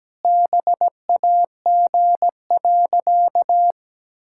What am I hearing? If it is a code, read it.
Morse code, heard as a keyed tone, about 17 wpm, 700 Hz. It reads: BAG.
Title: Pangram Challenge v2 - 17wpm